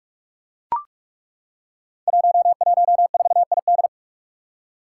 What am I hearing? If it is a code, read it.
Morse code, heard as a keyed tone, about 42 wpm, 700 Hz. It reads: 114ID
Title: Five Challenging Letter-Number Combinations 45wpm